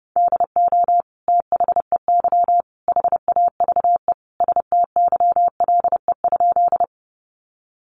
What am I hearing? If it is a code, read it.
Morse code, heard as a keyed tone, about 30 wpm, 700 Hz. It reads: DO THEY HAVE STYLE?
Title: Sentences from Top 800 Words 30wpm